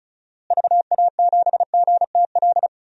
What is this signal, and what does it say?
Morse code, heard as a keyed tone, about 36 wpm, 700 Hz. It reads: VA7GTL
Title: DXCC-100 Call Signs - 6 Characters - Repeated 3x - 35wpm